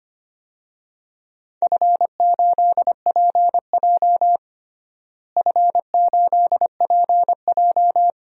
Morse code, heard as a keyed tone, about 25 wpm, 700 Hz. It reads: F8PJ F8PJ
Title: DXCC-100 Call Signs - 4 Characters - Repeated 2x - 25wpm